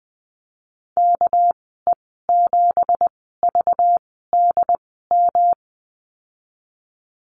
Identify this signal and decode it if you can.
Morse code, heard as a keyed tone, about 20 wpm, 700 Hz. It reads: K E 7 V D M
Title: 200 Lesson 32g - Letter k - Call Signs 20wpm